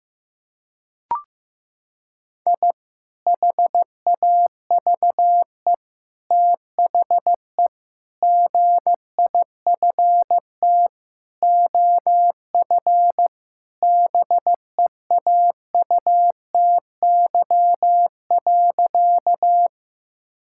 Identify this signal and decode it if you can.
Morse code, heard as a keyed tone, about 15 wpm, 700 Hz. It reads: I HAVE THE GIFT OF BEAUTY.
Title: Sentences from Top 1000 Words - Review Part 6 - 15wpm